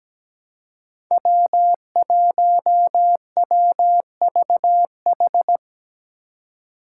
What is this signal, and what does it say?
Morse code, heard as a keyed tone, about 17 wpm, 700 Hz. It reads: W1WVH
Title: Five Challenging Letter-Number Combinations 17wpm